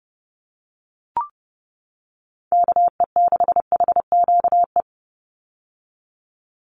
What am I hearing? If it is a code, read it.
Morse code, heard as a keyed tone, about 30 wpm, 700 Hz. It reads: KE6HQE
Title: USA Call Signs 30wpm